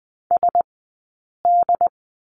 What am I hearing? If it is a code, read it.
Morse code, heard as a keyed tone, about 20 wpm, 700 Hz. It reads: S D
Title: 080 Lesson 14d - Number 2 - Two Characters 20wpm